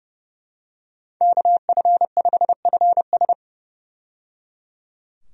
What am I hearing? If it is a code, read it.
Morse code, heard as a keyed tone, about 30 wpm, 700 Hz. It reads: KF5FS
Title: USA Call Signs - 5 Characters - 30wpm